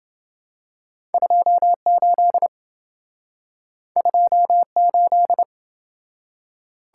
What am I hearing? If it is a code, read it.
Morse code, heard as a keyed tone, about 27 wpm, 700 Hz. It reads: 28 28